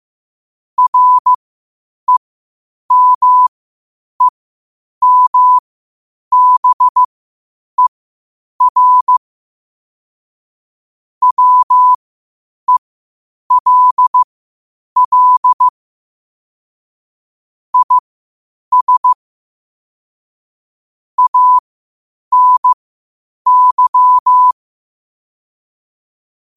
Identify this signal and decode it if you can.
Morse code, heard as a keyed tone, about 15 wpm, spaced as for 8 wpm, 1000 Hz. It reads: REMEMBER WELL IS ANY
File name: qotdfiles / 8wpm